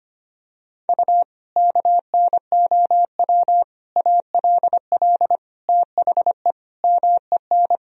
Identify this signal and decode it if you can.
Morse code, heard as a keyed tone, about 25 wpm, 700 Hz. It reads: U KNOW ALL THE MEN
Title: Sentences from Top 300 Words and Abbreviations 25wpm